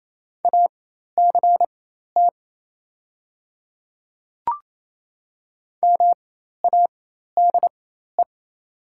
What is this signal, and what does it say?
Morse code, heard as a keyed tone, about 27 wpm, spaced as for 13 wpm, 690 Hz. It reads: ACT MADE
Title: 4x Character Spacing - Top 300 Words - 28wpm